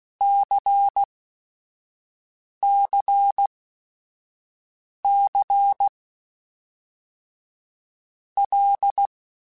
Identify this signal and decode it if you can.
Morse code, heard as a keyed tone, about 16 wpm, 800 Hz. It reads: C C C L
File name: Basic Sets / Basic Letter Groups Collective7 / LFCYQ@5